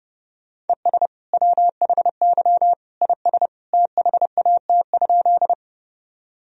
Morse code, heard as a keyed tone, about 30 wpm, 700 Hz. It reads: ES WHY IS THAT?
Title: Sentences from Top 200 Words and Abbreviations 30wpm